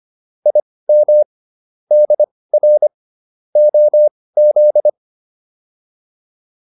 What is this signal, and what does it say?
Morse code, heard as a keyed tone, about 25 wpm, spaced as for 18 wpm, 600 Hz. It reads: IM DR OZ